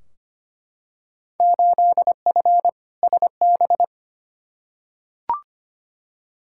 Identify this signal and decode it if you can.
Morse code, heard as a keyed tone, about 25 wpm, 700 Hz. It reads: 8F SB
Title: Field Day Exchanges Encore - 25wpm